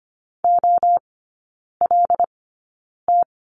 Morse code, heard as a keyed tone, about 25 wpm, spaced as for 9 wpm, 700 Hz. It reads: OLT
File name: course / 25:10wpm / 093 Lesson 16e - Letter c - Three Characters 25wpm